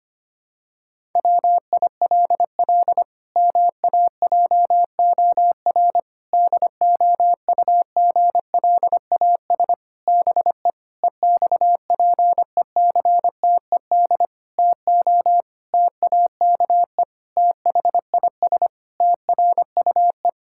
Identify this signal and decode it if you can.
Morse code, heard as a keyed tone, about 25 wpm, 700 Hz. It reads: WILL MAJOR DOUGLAS BE EXPECTED TO TAKE THIS TRUE
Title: Pangram Challenge v2 - 25wpm